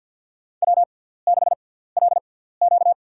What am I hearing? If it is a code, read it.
Morse code, heard as a keyed tone, about 54 wpm, spaced as for 17 wpm, 700 Hz. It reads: W6LQ
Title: USA Call Signs - 4 Characters - 6x Character Spacing - 50wpm